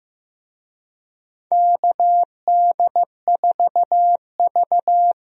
Morse code, heard as a keyed tone, about 15 wpm, 700 Hz. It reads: KD4V
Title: USA Call Signs - 4 Characters - 15wpm